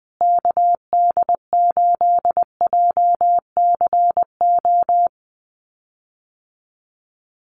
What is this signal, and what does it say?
Morse code, heard as a keyed tone, about 20 wpm, 700 Hz. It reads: KD8JCO
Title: USA Call Signs Repeated 3x 20wpm